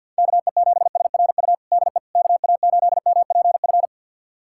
Morse code, heard as a keyed tone, about 51 wpm, 700 Hz. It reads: KE7SRU DE KA8MPF
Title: USA Callsign DE Callsign 50wpm